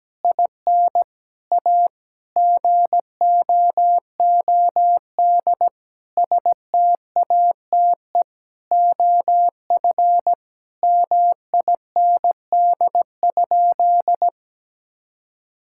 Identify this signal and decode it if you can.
Morse code, heard as a keyed tone, about 17 wpm, 700 Hz. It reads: IN A GOOD STATE OF MIND?
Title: Sentences from Top 1000 Words - Review Part 5 - 17wpm